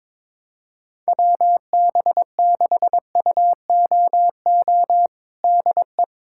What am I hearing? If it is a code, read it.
Morse code, heard as a keyed tone, about 21 wpm, 700 Hz. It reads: WB6UOO DE
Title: USA Callsign DE Callsign 22wpm